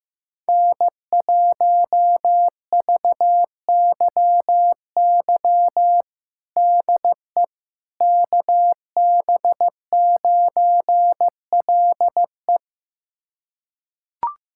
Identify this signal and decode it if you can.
Morse code, heard as a keyed tone, about 15 wpm, 700 Hz. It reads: N1VYY DE KB9LE
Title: USA Callsign DE Callsign - 5 Characters - 15wpm